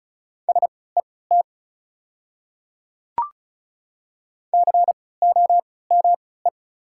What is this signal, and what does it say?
Morse code, heard as a keyed tone, about 36 wpm, spaced as for 20 wpm, 690 Hz. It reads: SET COME
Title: 3x Character Spacing - Top 300 Words - 35wpm